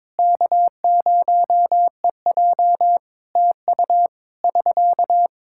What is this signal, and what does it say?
Morse code, heard as a keyed tone, about 21 wpm, 700 Hz. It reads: K0EJ TU <SK>
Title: Practice QSO - Exchange 4 - Ending - 22wpm